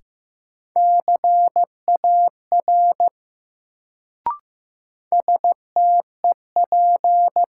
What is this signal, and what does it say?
Morse code, heard as a keyed tone, about 15 wpm, 700 Hz. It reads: CAR STEP
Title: Sets of 5 Words Encore - New Word Drill - 15wpm